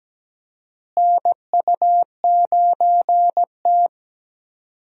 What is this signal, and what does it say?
Morse code, heard as a keyed tone, about 17 wpm, 700 Hz. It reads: NU9T
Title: USA Call Signs - 4 Characters - Repeated 2x - 17wpm